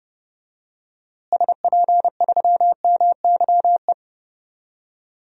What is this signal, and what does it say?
Morse code, heard as a keyed tone, about 30 wpm, 700 Hz. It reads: SP3MYE
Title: DXCC-100 Call Signs - 6 Characters - Repeated 3x - 30wpm